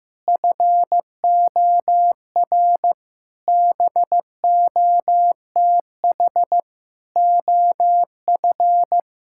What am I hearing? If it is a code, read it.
Morse code, heard as a keyed tone, about 15 wpm, 700 Hz. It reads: FOR BOTH OF
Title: Sentences from Top 1000 Words - Review Part 5 - 15wpm